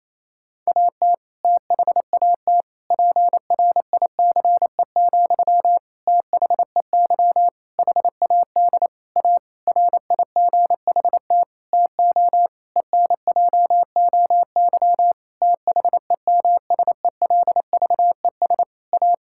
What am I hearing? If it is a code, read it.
Morse code, heard as a keyed tone, about 28 wpm, 700 Hz. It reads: AT THAT PRICE, THEY HAD A RIGHT TO ENJOY THEMSELVES A
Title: Sentences from Top 1000 Words - Review Part 5 - 28wpm